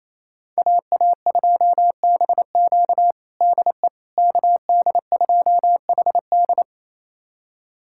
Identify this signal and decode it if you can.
Morse code, heard as a keyed tone, about 28 wpm, 700 Hz. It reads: AA2BQ DE KD2HD
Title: USA Callsign DE Callsign - 5 Characters - 28wpm